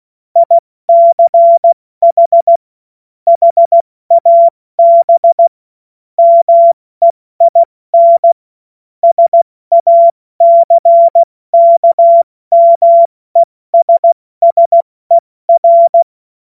Morse code, heard as a keyed tone, about 16 wpm, spaced as for 14 wpm, 680 Hz. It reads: ICH HAB MEIN SACKMESSER
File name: Morsekurs / Witze / Witz 14. Dezember 2021, 70 F80